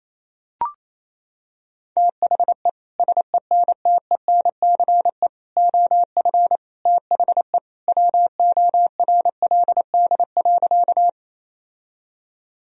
Morse code, heard as a keyed tone, about 28 wpm, 700 Hz. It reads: THE SENTENCE OF THE WORLD.
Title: Sentences from Top 1000 Words - Review Part 1 - 28wpm